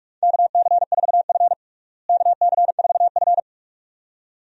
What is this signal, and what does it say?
Morse code, heard as a keyed tone, about 42 wpm, 700 Hz. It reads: KC4F KC4F
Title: USA Call Signs - 4 Characters - Repeated 2x - 45wpm